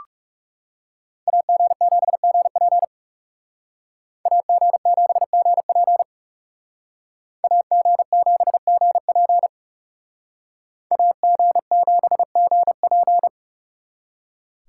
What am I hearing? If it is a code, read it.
Morse code, heard as a keyed tone, about 37 wpm, 700 Hz. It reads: AG7GP AG7GP AG7GP AG7GP
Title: Speed Racing - USA Call Signs 30wpm